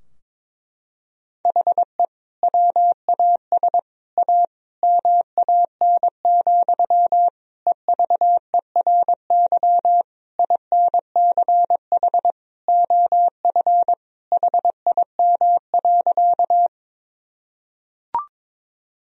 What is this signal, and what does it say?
Morse code, heard as a keyed tone, about 22 wpm, 700 Hz. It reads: HE WAS A MAN, EVERY INCH OF HIM.
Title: Sentences from Top 1000 Words - Review Part 3 - 22wpm